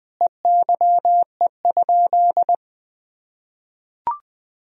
Morse code, heard as a keyed tone, about 20 wpm, 700 Hz. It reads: EYE?
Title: Sentences from Top 300 Words and Abbreviations 20wpm